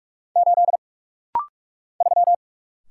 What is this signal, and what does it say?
Morse code, heard as a keyed tone, about 44 wpm, 690 Hz. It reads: 8 3
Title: Single Letter-Number - Rapid-Fire 45wpm